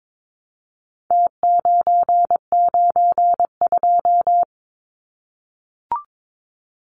Morse code, heard as a keyed tone, about 22 wpm, 700 Hz. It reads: T992
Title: Numbers - Contest Style - 4-Digit 22wpm